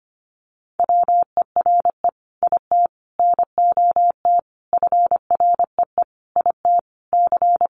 Morse code, heard as a keyed tone, about 25 wpm, 700 Hz. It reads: WERE IT NOT FREE IT C
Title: Sentences from Top 900 Words 25wpm